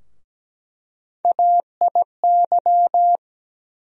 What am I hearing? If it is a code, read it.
Morse code, heard as a keyed tone, about 17 wpm, 700 Hz. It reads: AIY
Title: USA Call Sign Postfixes - 3 Letters 17wpm